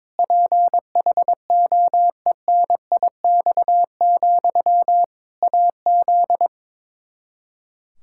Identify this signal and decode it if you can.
Morse code, heard as a keyed tone, about 22 wpm, 700 Hz. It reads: PHOENIX, AZ
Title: Practice QSO - Exchange 1 - Beginning - 22wpm